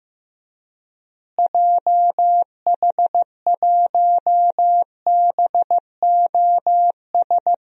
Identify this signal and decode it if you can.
Morse code, heard as a keyed tone, about 15 wpm, 700 Hz. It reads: JH1BOS
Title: DXCC-100 Call Signs - 6 Characters - Repeated 2x - 15wpm